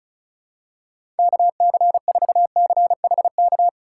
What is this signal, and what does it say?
Morse code, heard as a keyed tone, about 36 wpm, 700 Hz. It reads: KC4CHK
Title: USA Call Signs - 3x Character Spacing - 35wpm